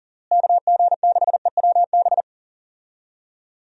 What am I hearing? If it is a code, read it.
Morse code, heard as a keyed tone, about 40 wpm, 700 Hz. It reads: KG6EWB
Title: USA Call Signs - 6 Characters - Repeated 3x - 40wpm